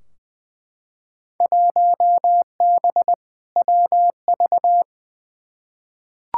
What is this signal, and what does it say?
Morse code, heard as a keyed tone, about 20 wpm, 700 Hz. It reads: 1B WV
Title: Field Day Exchanges Encore - 20wpm